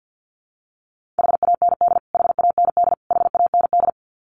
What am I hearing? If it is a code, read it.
Morse code, heard as a keyed tone, about 48 wpm, 700 Hz. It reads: HAND HAND HAND
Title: Top 500 Words Repeated 50wpm